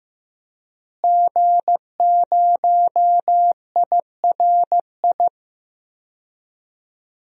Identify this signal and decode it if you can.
Morse code, heard as a keyed tone, about 15 wpm, 700 Hz. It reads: G0IRI
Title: DXCC-100 Call Signs - 5 Characters - Repeated 3x - 15wpm